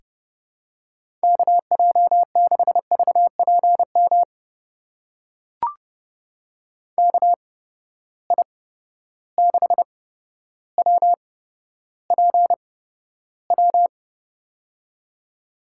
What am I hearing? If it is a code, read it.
Morse code, heard as a keyed tone, about 30 wpm, spaced as for 13 wpm, 700 Hz. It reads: KJ6VPM KI6WPW
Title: USA Call Signs - 6 Characters - 8x Character Spacing - 30wpm